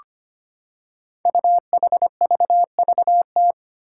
Morse code, heard as a keyed tone, about 25 wpm, 700 Hz. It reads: UHVVT
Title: Five Challenging Letter-Number Combinations 25wpm